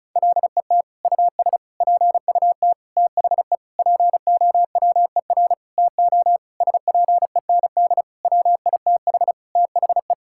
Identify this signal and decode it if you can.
Morse code, heard as a keyed tone, about 34 wpm, 700 Hz. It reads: LET US PUT THE POWER TO SPEND WITH THE
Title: Sentences from Top 1000 Words - Review Part 5 - 35wpm